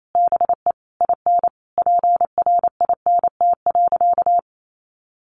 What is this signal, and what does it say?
Morse code, heard as a keyed tone, about 28 wpm, 700 Hz. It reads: BE IN PRINT.
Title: Sentences from Top 1400 Words 28wpm